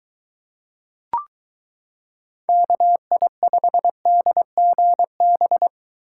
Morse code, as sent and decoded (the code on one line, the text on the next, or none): -.- .. ..... -.. --. -...
KI5DGB